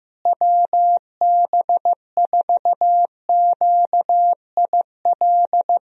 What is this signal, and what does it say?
Morse code, heard as a keyed tone, about 15 wpm, 700 Hz. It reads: WB4QIL
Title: USA Call Signs - 6 Characters - Repeated 3x - 15wpm